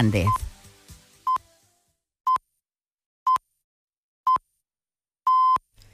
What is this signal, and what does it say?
Radio time signal (time pips), with speech. It is heard from 0.3 to 5.6 s.